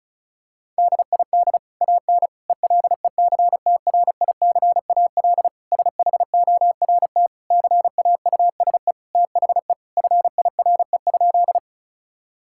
Morse code, heard as a keyed tone, about 36 wpm, 700 Hz. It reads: DID AN ELECTRICAL SHORT CAUSE THE FIRE?
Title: Sentences from Top 1000 Words - Review Part 6 - 35wpm